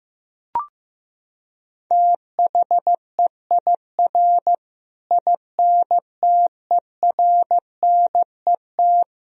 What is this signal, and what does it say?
Morse code, heard as a keyed tone, about 15 wpm, 700 Hz. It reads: THEIR INTERNET
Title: Sentences from Top 1000 Words - Review Part 4 - 15wpm